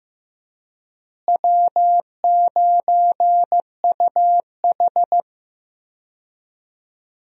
Morse code, heard as a keyed tone, about 15 wpm, 700 Hz. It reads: W9UH
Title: USA Call Signs - 4 Characters - Repeated 2x - 15wpm